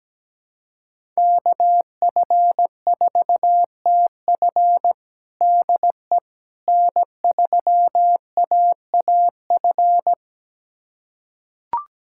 Morse code, heard as a keyed tone, about 17 wpm, 700 Hz. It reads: KF4TF DE N3AAF
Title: USA Callsign DE Callsign - 5 Characters - 17wpm